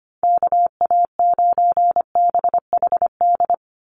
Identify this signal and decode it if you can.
Morse code, heard as a keyed tone, about 25 wpm, 700 Hz. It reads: KA9BHD
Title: Call Signs - 25wpm